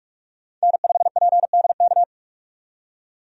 Morse code, heard as a keyed tone, about 47 wpm, 700 Hz. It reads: N5PDK
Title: USA Call Signs - 5 Characters - 45wpm